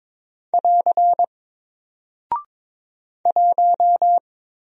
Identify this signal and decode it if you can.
Morse code, heard as a keyed tone, about 22 wpm, 700 Hz. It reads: <AR> 1
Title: Single Character (Letter:Number:Punctuation:Prosign) 22wpm